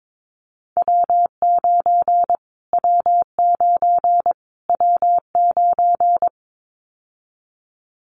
Morse code, heard as a keyed tone, about 22 wpm, 700 Hz. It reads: W9 W9 W9
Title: Callsign Prefixes for Top 25 DXCC - Repeated 3x - 22wpm